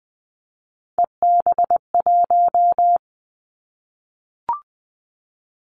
Morse code, heard as a keyed tone, about 20 wpm, 700 Hz. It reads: EB1
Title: Callsign Prefixes for Top 25 DXCC - Repeated 3x - 20wpm